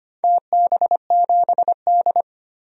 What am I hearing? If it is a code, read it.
Morse code, heard as a keyed tone, about 25 wpm, 700 Hz. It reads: TB7D
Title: DXCC-100 Call Signs - 4 Characters - 25wpm